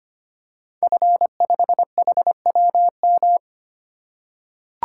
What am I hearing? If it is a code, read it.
Morse code, heard as a keyed tone, about 25 wpm, 700 Hz. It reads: F5HWM
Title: Code Groups - 5 Letter:Numbers - 25wpm